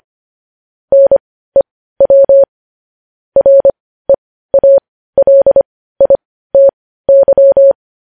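Morse code, heard as a keyed tone, about 25 wpm, spaced as for 15 wpm, 550 Hz. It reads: NEW REALITY
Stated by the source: Morse code news headlines sent at 15 wpm.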